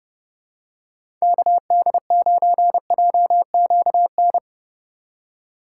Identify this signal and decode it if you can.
Morse code, heard as a keyed tone, about 30 wpm, 700 Hz. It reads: KD9JQN
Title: USA Call Signs - 6 Characters - 3x Character Spacing - 30wpm